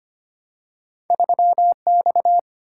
Morse code, heard as a keyed tone, about 25 wpm, 700 Hz. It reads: 3X
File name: Two Letter-Number Combinations - 25wpm